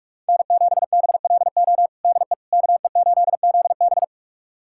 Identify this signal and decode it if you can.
Morse code, heard as a keyed tone, about 46 wpm, 700 Hz. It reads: N7BLO DE KE8ZB
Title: USA Callsign DE Callsign - 5 Characters - 45wpm